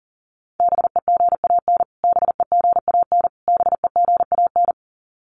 Morse code, heard as a keyed tone, about 40 wpm, 700 Hz. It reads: BEGAN BEGAN BEGAN
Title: Top 500 Words Repeated 40wpm